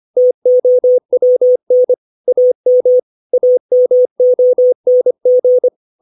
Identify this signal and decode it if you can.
Morse code, heard as a keyed tone, about 25 wpm, 500 Hz. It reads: TOWN AM AMONG
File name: 632 words / 632 Words 25WPM